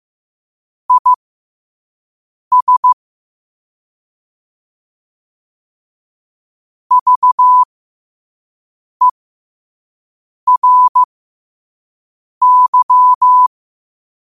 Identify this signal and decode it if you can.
Morse code, heard as a keyed tone, about 15 wpm, spaced as for 5 wpm, 1000 Hz. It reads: IS VERY